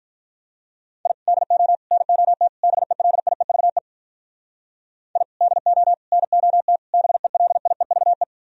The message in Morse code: I DO NOT BELIEVE I DO NOT BELIEVE

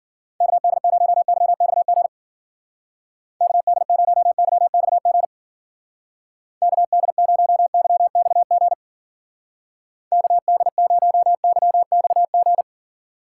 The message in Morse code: KD0YXG KD0YXG KD0YXG KD0YXG

